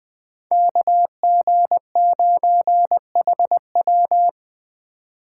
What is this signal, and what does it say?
Morse code, heard as a keyed tone, about 20 wpm, 700 Hz. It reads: KG9HW
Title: Speed Racing - USA 2x2 Call Signs 17wpm